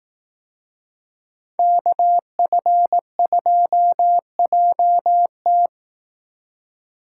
Morse code, heard as a keyed tone, about 18 wpm, 700 Hz. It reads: KF2JT
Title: Speed Racing - USA 2x2 Call Signs 15wpm